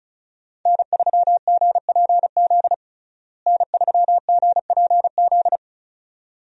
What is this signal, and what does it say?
Morse code, heard as a keyed tone, about 34 wpm, 700 Hz. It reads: N3GPZ N3GPZ
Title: USA Call Signs - 5 Characters - Repeated 2x - 35wpm